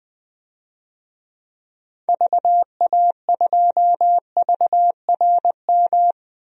Morse code, heard as a keyed tone, about 20 wpm, 700 Hz. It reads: VA2VRM
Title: DXCC-100 Call Signs - 6 Characters - 20wpm